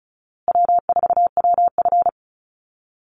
Morse code, heard as a keyed tone, about 34 wpm, 700 Hz. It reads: W4WF
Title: Easy Call Signs Repeated 3x - 35wpm